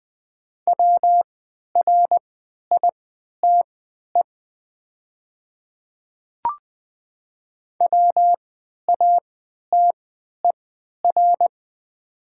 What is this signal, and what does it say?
Morse code, heard as a keyed tone, about 20 wpm, spaced as for 11 wpm, 690 Hz. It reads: WRITE WATER
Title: 3x Character Spacing - Top 200 Words - 20wpm